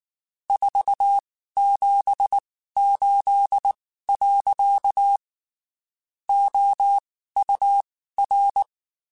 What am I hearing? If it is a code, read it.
Morse code, heard as a keyed tone, about 19 wpm, spaced as for 14 wpm, 800 Hz. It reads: 478. OUR
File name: (spaced) / QSOs Spaced-mp3s / WD0FBT @18 sp